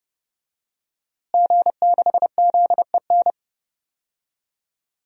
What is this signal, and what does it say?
Morse code, heard as a keyed tone, about 30 wpm, 700 Hz. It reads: G6ZEN
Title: DXCC-100 Call Signs - 5 Characters - Repeated 3x - 30wpm